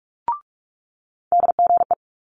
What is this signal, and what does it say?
Morse code, heard as a keyed tone, about 46 wpm, 700 Hz. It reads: DGE
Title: Multi-letter Phonemes Quick Fire 45wpm